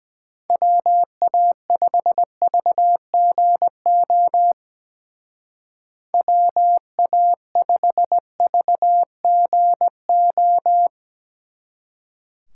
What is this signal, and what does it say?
Morse code, heard as a keyed tone, about 20 wpm, 700 Hz. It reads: WA5VGO WA5VGO
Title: Speed Racing - USA 2x3 Call Signs 17wpm